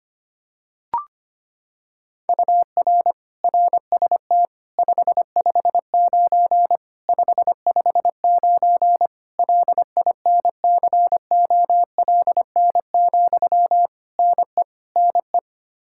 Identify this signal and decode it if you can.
Morse code, heard as a keyed tone, about 25 wpm, 700 Hz. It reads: UR RST 559 559 LINCOLN, NE NE
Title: Practice QSO - Exchange 1 - Beginning - 25wpm